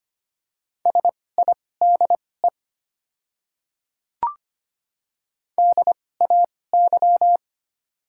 Morse code, heard as a keyed tone, about 25 wpm, spaced as for 18 wpm, 690 Hz. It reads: SIDE DAY
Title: 2x Character Spacing - Top 100 Words - 25wpm